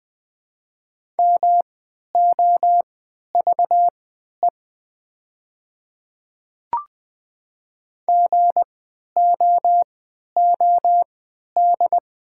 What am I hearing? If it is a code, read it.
Morse code, heard as a keyed tone, about 20 wpm, spaced as for 11 wpm, 700 Hz. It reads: MOVE GOOD
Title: 3x Character Spacing - Top 200 Words - 20wpm